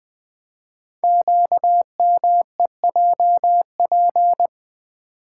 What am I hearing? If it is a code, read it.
Morse code, heard as a keyed tone, about 20 wpm, 700 Hz. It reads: QMEJP
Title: Code Groups - 5 Letter:Numbers - 20wpm